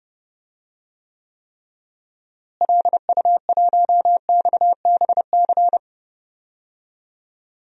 Morse code, heard as a keyed tone, about 30 wpm, 700 Hz. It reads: LU1XBC
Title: DXCC-100 Call Signs - 6 Characters - Repeated 3x - 30wpm